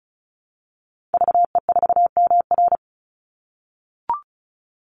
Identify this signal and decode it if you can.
Morse code, heard as a keyed tone, about 34 wpm, 700 Hz. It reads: VE4MR
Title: Call Signs - 35wpm